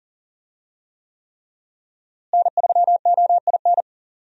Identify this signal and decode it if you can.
Morse code, heard as a keyed tone, about 40 wpm, 700 Hz. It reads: N3OIN
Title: USA Call Signs - 5 Characters - Repeated 3x - 40wpm